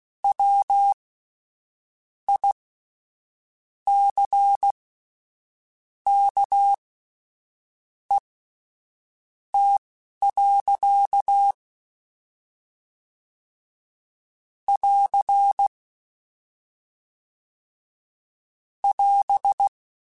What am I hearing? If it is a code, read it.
Morse code, heard as a keyed tone, about 16 wpm, spaced as for 6 wpm, 800 Hz. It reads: WICKET. <AR> <AS>
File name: Misc / Misc-mp3s / WordPairs sp @10